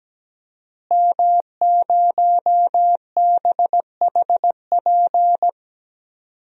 Morse code, heard as a keyed tone, about 17 wpm, 700 Hz. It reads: M0BHP
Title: DXCC-100 Call Signs - 5 Characters - Repeated 3x - 17wpm